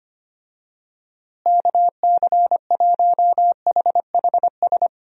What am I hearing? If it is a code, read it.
Morse code, heard as a keyed tone, about 25 wpm, 700 Hz. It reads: KC1HHS
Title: USA Call Signs - 6 Characters - Repeated 3x - 25wpm